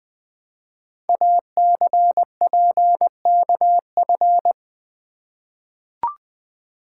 Morse code, heard as a keyed tone, about 20 wpm, 700 Hz. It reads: ACPKF
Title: Code Groups - 5 Letter:Numbers - 20wpm